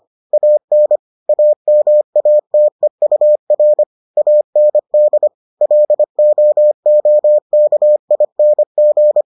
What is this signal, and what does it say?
Morse code, heard as a keyed tone, about 25 wpm, 600 Hz. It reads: AN AMATEUR AND LOOKING